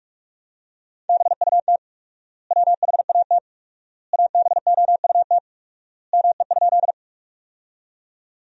Morse code, heard as a keyed tone, about 46 wpm, 700 Hz. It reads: BUT WHAT ABOUT ME?
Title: 4x Word-Spacing - Sentences from Top 200 Words - 45wpm